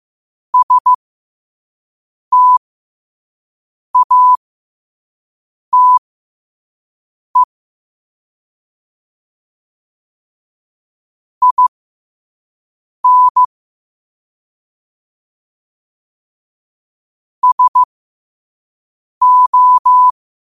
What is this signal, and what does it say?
Morse code, heard as a keyed tone, about 15 wpm, spaced as for 5 wpm, 1000 Hz. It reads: STATE IN SO
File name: qotdfiles / 5wpm